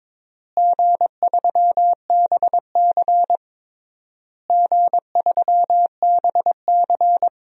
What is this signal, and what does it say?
Morse code, heard as a keyed tone, about 22 wpm, 700 Hz. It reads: G3BC G3BC
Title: DXCC-100 Call Signs - 4 Characters - Repeated 3x - 22wpm